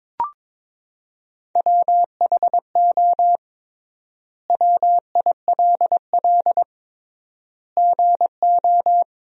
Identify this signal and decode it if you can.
Morse code, heard as a keyed tone, about 22 wpm, 700 Hz. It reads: WHO WILL GO
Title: 3x Word-Spacing - Sentences from Top 100 Words - 22wpm